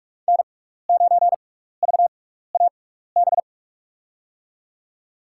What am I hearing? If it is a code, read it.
Morse code, heard as a keyed tone, about 46 wpm, spaced as for 15 wpm, 700 Hz. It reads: N9VAB
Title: USA Call Signs - 6x Character Spacing - 45wpm